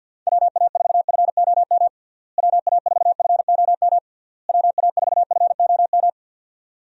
Morse code, heard as a keyed tone, about 46 wpm, 700 Hz. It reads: WA4FOM WA4FOM WA4FOM
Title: USA Call Signs - 6 Characters - Repeated 3x - 50wpm